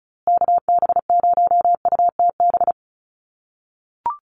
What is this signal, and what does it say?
Morse code, heard as a keyed tone, about 34 wpm, 700 Hz. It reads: KB0UTB
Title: USA Call Signs Repeated 3x 35wpm